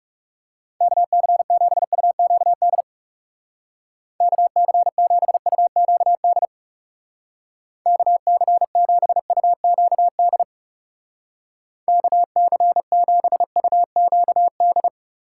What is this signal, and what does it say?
Morse code, heard as a keyed tone, about 37 wpm, 700 Hz. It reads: KC7UQD KC7UQD KC7UQD KC7UQD
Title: Speed Racing - USA 2x3 Call Signs 30wpm